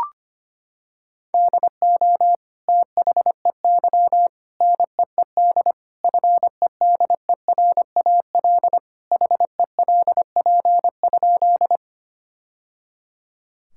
Morse code, heard as a keyed tone, about 25 wpm, 700 Hz. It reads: DO THEY NEED FEDERAL HELP?
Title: Sentences from Top 1000 Words - Review Part 5 - 25wpm